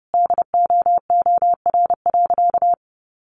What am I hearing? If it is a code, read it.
Morse code, heard as a keyed tone, about 30 wpm, 700 Hz. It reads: DOOR.
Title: Sentences from Top 400 Words 30wpm